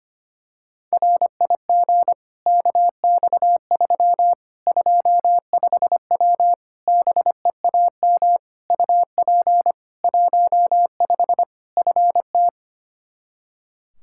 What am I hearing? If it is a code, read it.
Morse code, heard as a keyed tone, about 25 wpm, 700 Hz. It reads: RIG KX3 25W BEAM UP 15 FT
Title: K1USN SST QSO Practice 25wpm